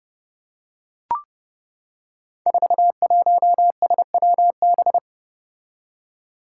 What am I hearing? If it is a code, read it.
Morse code, heard as a keyed tone, about 30 wpm, 700 Hz. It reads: 41SWB
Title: Five Challenging Letter-Number Combinations 30wpm